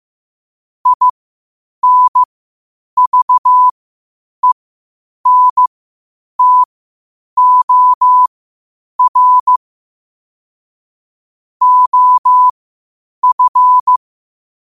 Morse code, heard as a keyed tone, about 15 wpm, spaced as for 8 wpm, 1000 Hz. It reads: INVENTOR OF
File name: qotdfiles / 8wpm